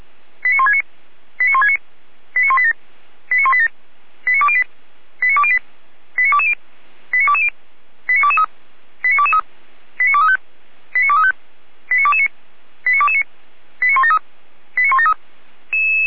Five-tone selective call (ZVEI), heard as quick runs of five tones, tones of about 70 ms. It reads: 89169 89169 89187 89187 89208 89208 89220 89220 89223 89223 89235 89235 89229 89229 89173 89173